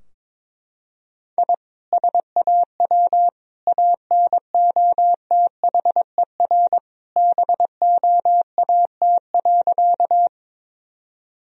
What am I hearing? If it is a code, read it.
Morse code, heard as a keyed tone, about 21 wpm, 700 Hz. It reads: I SAW ANOTHER BOAT.
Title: Sentences from Top 1000 Words - Review Part 3 - 22wpm